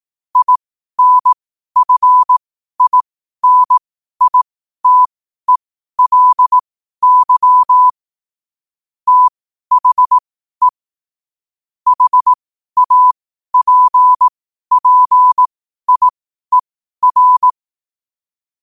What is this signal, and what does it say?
Morse code, heard as a keyed tone, about 18 wpm, spaced as for 12 wpm, 1000 Hz. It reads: INFINITELY THE HAPPIER
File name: qotdfiles / 12wpm